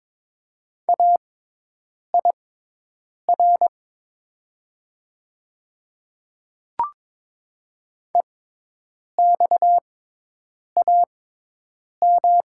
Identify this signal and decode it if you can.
Morse code, heard as a keyed tone, about 23 wpm, spaced as for 8 wpm, 700 Hz. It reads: AIR EXAM
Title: 6x Character Spacing - Top 300 Words - 22wpm